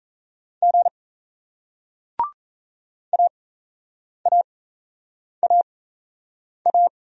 Morse code, heard as a keyed tone, about 38 wpm, spaced as for 8 wpm, 700 Hz. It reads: G AAAA